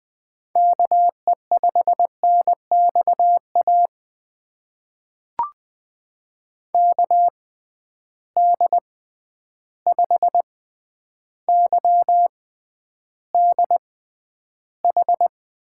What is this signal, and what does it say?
Morse code, heard as a keyed tone, about 20 wpm, spaced as for 10 wpm, 700 Hz. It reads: KE5NXA KD5YDH
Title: USA Call Signs - 6 Characters - 6x Character Spacing - 20wpm